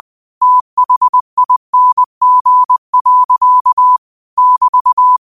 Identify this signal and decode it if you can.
Morse code, heard as a keyed tone, about 20 wpm, 1000 Hz. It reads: THING. <BT>
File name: qotdfiles / 20wpm